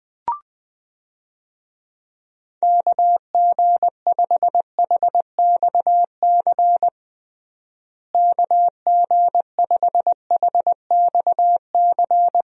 Morse code, heard as a keyed tone, about 20 wpm, 700 Hz. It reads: KG5HXC KG5HXC
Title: USA Call Signs - 6 Characters - Repeated 3x - 20wpm